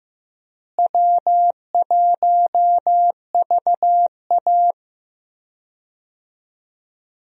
Morse code, heard as a keyed tone, about 15 wpm, 700 Hz. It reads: W1VA
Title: USA Call Signs - 4 Characters - Repeated 3x - 15wpm